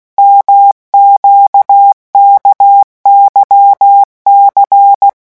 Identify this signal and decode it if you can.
Morse code, heard as a keyed tone, about 16 wpm, 800 Hz. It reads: MQKYC
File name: morse mp3 / revision2-vitesse 15 mots